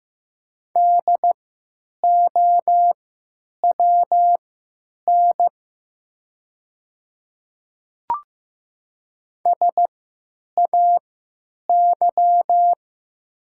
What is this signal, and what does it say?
Morse code, heard as a keyed tone, about 15 wpm, spaced as for 9 wpm, 700 Hz. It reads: DOWN SAY
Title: 3x Character Spacing - Top 300 Words - 15wpm